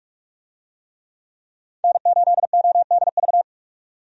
Morse code, heard as a keyed tone, about 46 wpm, 700 Hz. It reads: N8ODV
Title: USA Call Signs - 5 Characters - 45wpm